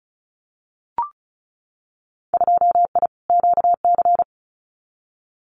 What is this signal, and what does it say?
Morse code, heard as a keyed tone, about 36 wpm, 700 Hz. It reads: 2I QC
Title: Winter Field Day Exchanges 35wpm